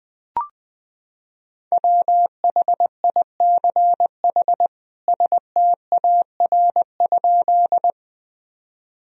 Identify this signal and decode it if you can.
Morse code, heard as a keyed tone, about 20 wpm, 700 Hz. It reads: WHICH STAR?
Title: Sentences from Top 1000 Words - Review Part 2 - 20wpm